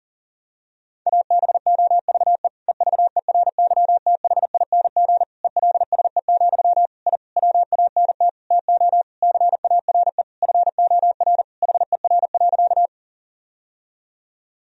Morse code, heard as a keyed tone, about 40 wpm, 700 Hz. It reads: ABOVE EVERYTHING ELSE, I WANT TO CARE FOR HER.
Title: Sentences from Top 1000 Words - Review Part 2 - 40wpm